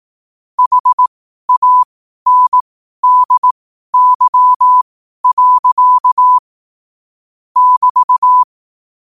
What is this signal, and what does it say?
Morse code, heard as a keyed tone, about 18 wpm, spaced as for 12 wpm, 1000 Hz. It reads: HANDY. <BT>
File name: qotdfiles / 12wpm